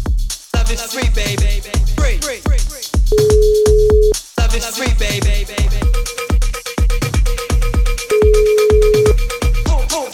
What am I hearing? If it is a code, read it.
Ringback tone (Europe, 425 Hz), with speech.